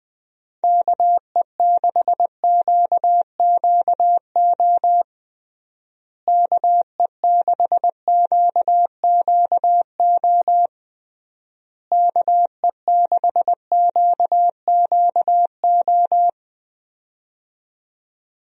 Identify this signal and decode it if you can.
Morse code, heard as a keyed tone, about 21 wpm, 700 Hz. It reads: KE6QQO KE6QQO KE6QQO
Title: USA Call Signs - 6 Characters - Repeated 3x - 20wpm